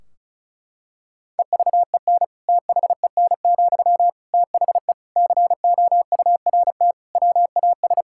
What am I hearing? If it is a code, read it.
Morse code, heard as a keyed tone, about 36 wpm, 700 Hz. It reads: EVEN THEN, THE COURT WAS
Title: Sentences from Top 1600 Words 35wpm